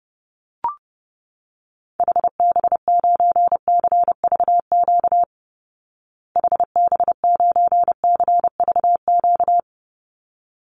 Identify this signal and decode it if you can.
Morse code, heard as a keyed tone, about 30 wpm, 700 Hz. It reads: HB9CVQ HB9CVQ
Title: Call Signs Repeated 3x - 30wpm